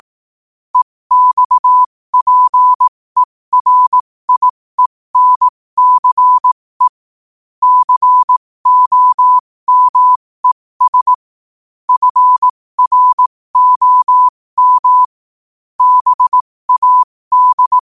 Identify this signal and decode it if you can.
Morse code, heard as a keyed tone, about 18 wpm, spaced as for 15 wpm, 1000 Hz. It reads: EXPERIENCE COMES FROM BAD
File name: qotdfiles / 15wpm